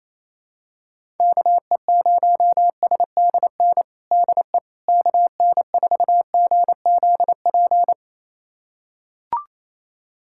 Morse code, heard as a keyed tone, about 28 wpm, 700 Hz. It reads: KE0SDN DE KN4GZP
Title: USA Callsign DE Callsign - 6 Characters - 28wpm